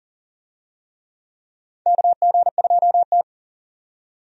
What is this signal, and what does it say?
Morse code, heard as a keyed tone, about 40 wpm, 700 Hz. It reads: KG2T
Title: USA Call Signs - 4 Characters - 40wpm